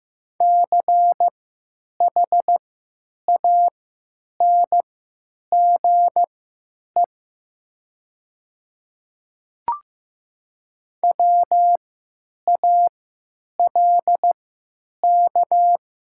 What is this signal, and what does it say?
Morse code, heard as a keyed tone, about 15 wpm, spaced as for 9 wpm, 700 Hz. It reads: CHANGE WALK
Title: 3x Character Spacing - Top 500 Words - 15wpm